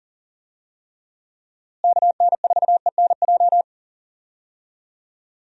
Morse code, heard as a keyed tone, about 40 wpm, 700 Hz. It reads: KN4ENJ
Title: USA Call Signs - 6 Characters - Repeated 2x - 40wpm